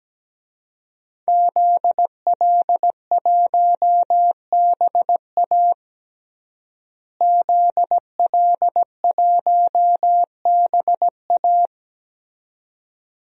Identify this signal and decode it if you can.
Morse code, heard as a keyed tone, about 17 wpm, 700 Hz. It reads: ZL1BA ZL1BA
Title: DXCC-100 Call Signs - 5 Characters - Repeated 2x - 17wpm